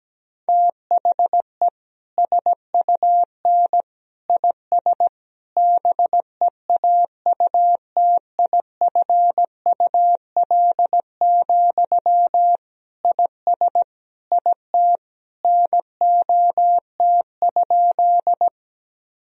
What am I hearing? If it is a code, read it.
Morse code, heard as a keyed tone, about 17 wpm, 700 Hz. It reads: THE SUN IS BEAUTIFUL, IS IT NOT?
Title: Sentences from Top 1000 Words - Review Part 1 - 17wpm